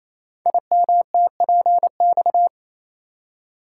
Morse code, heard as a keyed tone, about 27 wpm, 700 Hz. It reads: IMTPX
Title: Code Groups - 5 Letters - 28wpm